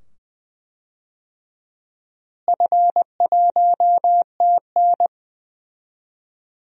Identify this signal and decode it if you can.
Morse code, heard as a keyed tone, about 20 wpm, 700 Hz. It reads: F1TN
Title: DXCC-100 Call Signs - 4 Characters - Repeated 2x - 20wpm